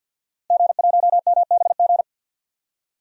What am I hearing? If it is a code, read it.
Morse code, heard as a keyed tone, about 51 wpm, 700 Hz. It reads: G1MBG
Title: DXCC-100 Call Signs - 5 Characters - Repeated 3x - 50wpm